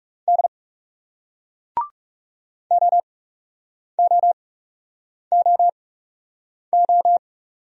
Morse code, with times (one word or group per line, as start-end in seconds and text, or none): 0.3-0.5 D
2.7-7.2 OOOO